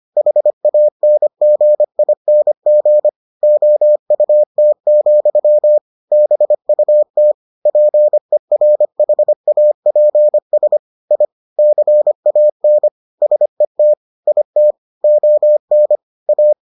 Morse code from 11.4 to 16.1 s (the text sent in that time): CAN SET IT ON